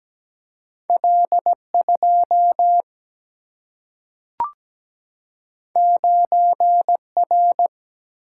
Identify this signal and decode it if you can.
Morse code, heard as a keyed tone, about 17 wpm, 700 Hz. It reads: L 2 9 R
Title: Single-Letter Single-Number 17wpm